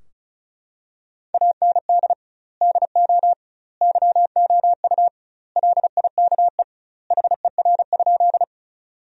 Morse code, heard as a keyed tone, about 34 wpm, 700 Hz. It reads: AND DO YOU LIKE HER?
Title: 2x Word-Spacing - Sentences from Top 100 Words - 35wpm